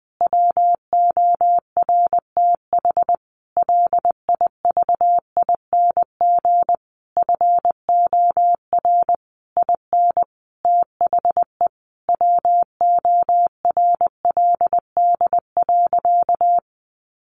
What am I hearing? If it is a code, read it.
Morse code, heard as a keyed tone, about 20 wpm, 700 Hz. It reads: WORTH LIVING FOR IN THE WORLD.
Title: Sentences from Top 1100 Words 20wpm